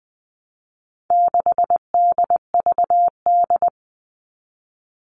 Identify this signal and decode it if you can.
Morse code, heard as a keyed tone, about 20 wpm, 700 Hz. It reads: 6DVD
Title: All Cut Numbers - 4-Digit 20wpm